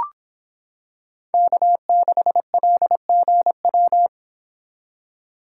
Morse code, heard as a keyed tone, about 25 wpm, 700 Hz. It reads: K6LGW